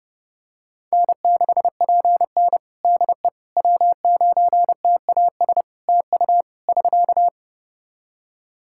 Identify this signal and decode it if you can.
Morse code, heard as a keyed tone, about 30 wpm, 700 Hz. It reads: N6PN DE W9TAS TU <SK>
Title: Practice QSO - Exchange 4 - Ending - 30wpm